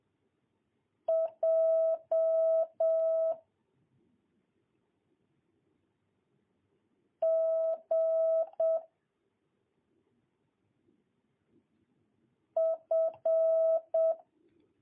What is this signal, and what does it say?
Morse code, heard as a keyed tone, about 7 wpm, 650 Hz. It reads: J G F
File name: jh8cbh / 2022cw